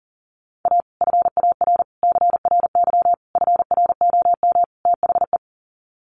Morse code, heard as a keyed tone, about 40 wpm, 700 Hz. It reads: A FAR CRY FROM THE
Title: Sets of 5 Words Encore 40wpm